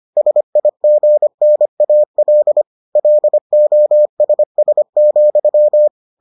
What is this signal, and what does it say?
Morse code, heard as a keyed tone, about 25 wpm, 600 Hz. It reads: SIGNAL LOSS,